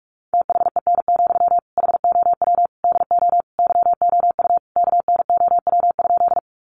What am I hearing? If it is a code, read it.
Morse code, heard as a keyed tone, about 44 wpm, 700 Hz. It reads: THEN, HOW DO YOU KNOW?
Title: Sentences from Top 100 Words 45wpm